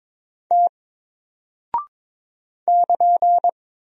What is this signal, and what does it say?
Morse code, heard as a keyed tone, about 22 wpm, 700 Hz. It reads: T <KN>
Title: Single Character (Letter:Number:Punctuation:Prosign) 22wpm